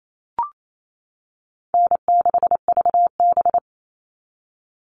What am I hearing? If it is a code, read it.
Morse code, heard as a keyed tone, about 28 wpm, 700 Hz. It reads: N6VB
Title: All Cut Numbers - 4-Digit 28wpm